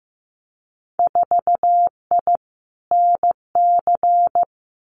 Morse code, heard as a keyed tone, about 15 wpm, 700 Hz. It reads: 4I NC
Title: Winter Field Day Exchanges 15wpm